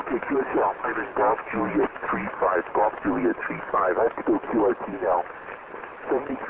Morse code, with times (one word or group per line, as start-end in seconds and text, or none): none